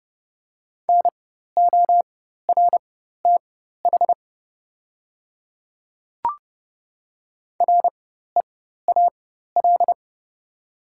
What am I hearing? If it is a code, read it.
Morse code, heard as a keyed tone, about 30 wpm, spaced as for 14 wpm, 700 Hz. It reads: NORTH REAL
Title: 4x Character Spacing - Top 300 Words - 30wpm